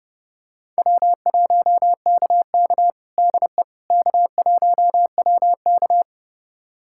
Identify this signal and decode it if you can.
Morse code, heard as a keyed tone, about 30 wpm, 700 Hz. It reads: W1KK DE K1WK
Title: USA Callsign DE Callsign - 4 Characters - 30wpm